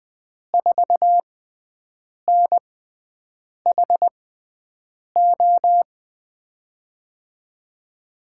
Morse code, heard as a keyed tone, about 21 wpm, spaced as for 7 wpm, 700 Hz. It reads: 4NHO
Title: USA Call Signs - 6x Character Spacing - 20wpm